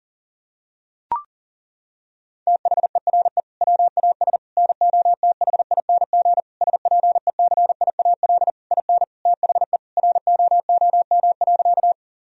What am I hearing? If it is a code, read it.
Morse code, heard as a keyed tone, about 40 wpm, 700 Hz. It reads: THERE WAS NOTHING SPECIAL IN THE ROOM.
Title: Sentences from Top 1000 Words - Review Part 3 - 40wpm